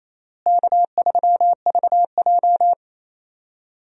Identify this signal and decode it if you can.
Morse code, heard as a keyed tone, about 28 wpm, 700 Hz. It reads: K3VJ
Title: USA Call Signs - 4 Characters - Repeated 3x - 28wpm